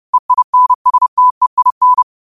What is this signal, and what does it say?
Morse code, heard as a keyed tone, about 30 wpm, 1000 Hz. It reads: EINSTEIN
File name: 30wpm